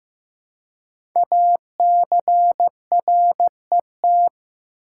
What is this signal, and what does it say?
Morse code, heard as a keyed tone, about 15 wpm, 700 Hz. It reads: ACRET
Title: Code Groups - 5 Letters - 15wpm